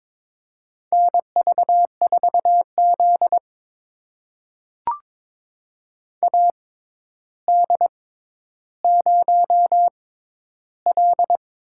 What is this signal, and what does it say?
Morse code, heard as a keyed tone, about 22 wpm, spaced as for 11 wpm, 700 Hz. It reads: NV4Z AD0L
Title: USA Call Signs - 4 Characters - 6x Character Spacing - 22wpm